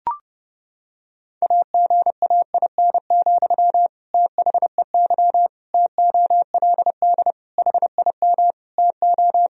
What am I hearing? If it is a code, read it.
Morse code, heard as a keyed tone, about 32 wpm, 700 Hz. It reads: AGAIN, THEY TOLD HIM TO
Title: Sentences from Top 1000 Words - Review Part 3 - 30wpm